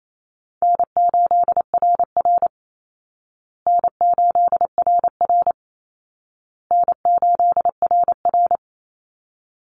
Morse code, heard as a keyed tone, about 28 wpm, 700 Hz. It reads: N8RR N8RR N8RR
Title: Easy Call Signs Repeated 3x - 28wpm